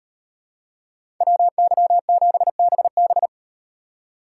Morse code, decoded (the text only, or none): WY7BB